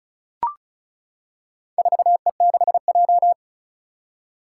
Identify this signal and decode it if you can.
Morse code, heard as a keyed tone, about 34 wpm, 700 Hz. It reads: 4E6J
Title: Four Challenging Letter-Number Combinations 35wpm